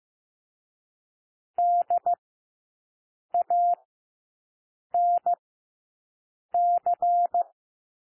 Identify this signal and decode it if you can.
Morse code, heard as a keyed tone, about 15 wpm, 700 Hz. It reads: D A N C
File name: cwmp3 / W1 SOUND WORDS 15 wpm